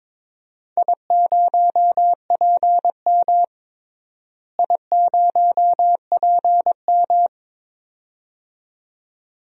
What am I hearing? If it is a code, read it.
Morse code, heard as a keyed tone, about 21 wpm, 700 Hz. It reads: I0PM I0PM
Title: DXCC-100 Call Signs - 4 Characters - Repeated 2x - 22wpm